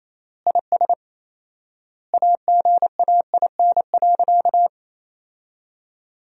Morse code, heard as a keyed tone, about 28 wpm, 700 Hz. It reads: IS AGAIN.
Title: 4x Word-Spacing - Sentences from Top 200 Words - 28wpm